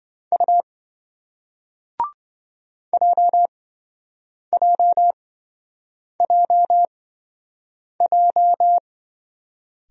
Morse code, heard as a keyed tone, about 24 wpm, spaced as for 7 wpm, 700 Hz. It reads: U JJJJ
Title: Speed-Racing Single Letters 20wpm